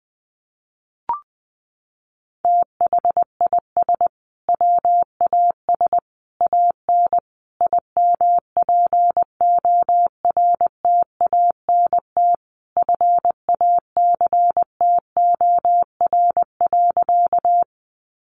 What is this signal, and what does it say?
Morse code, heard as a keyed tone, about 20 wpm, 700 Hz. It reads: THIS WAS AN IMPORTANT FACTOR.
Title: Sentences from Top 1300 Words 20wpm